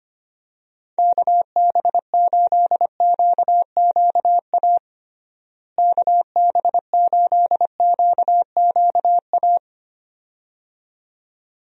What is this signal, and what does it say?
Morse code, heard as a keyed tone, about 25 wpm, 700 Hz. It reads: KB8QQA KB8QQA
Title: USA Call Signs - 6 Characters - Repeated 2x - 25wpm